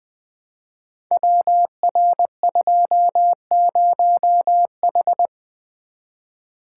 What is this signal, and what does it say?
Morse code, heard as a keyed tone, about 20 wpm, 700 Hz. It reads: WR20H
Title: Code Groups - 5 Letter:Numbers - 20wpm